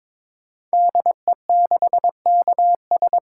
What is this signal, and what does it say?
Morse code, heard as a keyed tone, about 22 wpm, 700 Hz. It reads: DE6KS
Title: Code Groups - 5 Letter:Numbers - 22wpm